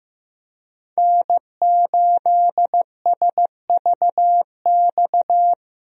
Morse code, heard as a keyed tone, about 15 wpm, 700 Hz. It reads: N8SVX
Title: USA Call Signs - 5 Characters - Repeated 2x - 15wpm